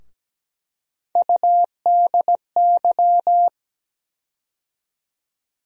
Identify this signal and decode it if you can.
Morse code, heard as a keyed tone, about 17 wpm, 700 Hz. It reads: UDY
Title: USA Call Sign Postfixes - 3 Letters 17wpm